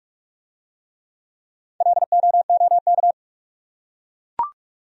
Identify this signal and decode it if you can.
Morse code, heard as a keyed tone, about 43 wpm, 700 Hz. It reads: LOOK